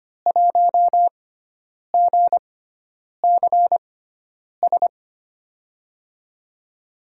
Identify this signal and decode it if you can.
Morse code, heard as a keyed tone, about 25 wpm, spaced as for 9 wpm, 700 Hz. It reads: 1GCS
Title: USA Call Signs - 6x Character Spacing - 25wpm